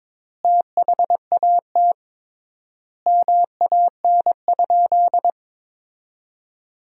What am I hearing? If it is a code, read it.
Morse code, heard as a keyed tone, about 21 wpm, 700 Hz. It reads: THAT MAN?
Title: 3x Word-Spacing - Sentences from Top 200 Words - 22wpm